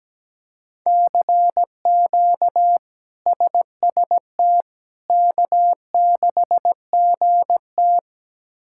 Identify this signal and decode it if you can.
Morse code, heard as a keyed tone, about 17 wpm, 700 Hz. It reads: CQ SST K6GT
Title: K1USN SST QSO Practice - 17wpm